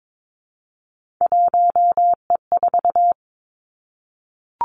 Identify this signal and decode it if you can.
Morse code, heard as a keyed tone, about 22 wpm, 700 Hz. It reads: 1E4
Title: Three Challenging Letter-Number Combinations 22wpm